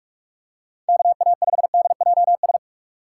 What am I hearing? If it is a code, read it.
Morse code, heard as a keyed tone, about 40 wpm, 700 Hz. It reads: KA5DJS